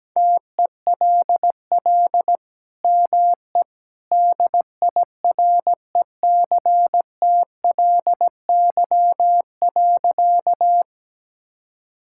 Morse code, heard as a keyed tone, about 17 wpm, 700 Hz. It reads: TELL ME DIRECTLY.
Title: Sentences from Top 1000 Words - Review Part 5 - 17wpm